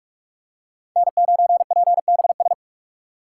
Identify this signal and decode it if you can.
Morse code, heard as a keyed tone, about 46 wpm, 700 Hz. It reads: N9PBS
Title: USA Call Signs - 5 Characters - 3x Character Spacing - 45wpm